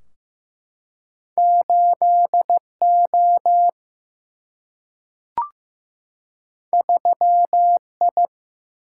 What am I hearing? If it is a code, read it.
Morse code, heard as a keyed tone, about 15 wpm, 700 Hz. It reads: 8O 3I
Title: Single-Letter Single-Number 15wpm